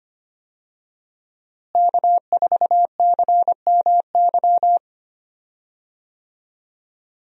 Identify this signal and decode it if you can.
Morse code, heard as a keyed tone, about 25 wpm, 700 Hz. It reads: K4CMY
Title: USA Call Signs - 5 Characters - Repeated 2x - 25wpm